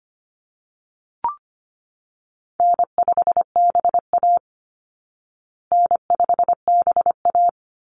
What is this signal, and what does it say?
Morse code, heard as a keyed tone, about 25 wpm, 700 Hz. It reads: N5BA N5BA
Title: Easy Call Signs Repeated 3x - 25wpm